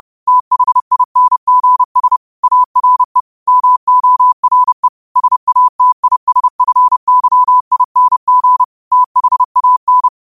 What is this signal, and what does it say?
Morse code, heard as a keyed tone, about 30 wpm, 1000 Hz. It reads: THINGS ARE MORE SATISFYING THAN
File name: qotdfiles / 30wpm